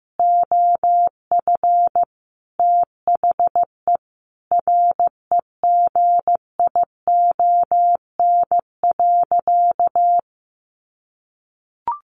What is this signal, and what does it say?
Morse code, heard as a keyed tone, about 15 wpm, 700 Hz. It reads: OF THE REGION.
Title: Sentences from Top 1500 Words 15wpm